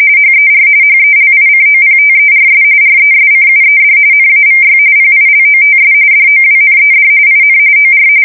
RTTY, about 45.45 baud. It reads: HER HELICOPTERS RESCUED NAVY PILOTS DOWNED IN OR
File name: fsk170 / USS-New-Jersey